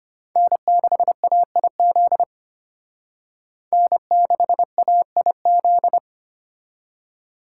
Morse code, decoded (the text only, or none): N6AIZ N6AIZ